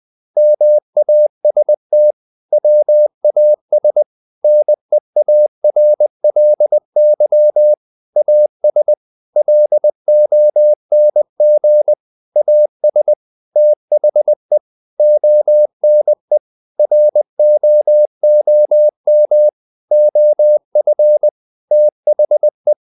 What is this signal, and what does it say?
Morse code, heard as a keyed tone, about 20 wpm, 600 Hz. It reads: MAST WAS NEARLY AS LONG AS THE ONE ROOM OF THE
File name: the old man and the sea / the old man and the sea part 1 20wpm